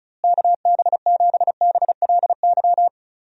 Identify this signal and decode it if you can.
Morse code, heard as a keyed tone, about 34 wpm, 700 Hz. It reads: KB7BLY
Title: USA Call Signs - 6 Characters - Repeated 3x - 35wpm